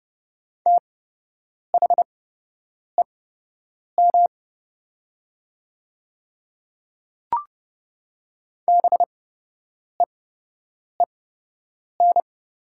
Morse code, heard as a keyed tone, about 30 wpm, spaced as for 8 wpm, 700 Hz. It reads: THEM BEEN
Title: 8x Character Spacing - Top 100 Words - 30wpm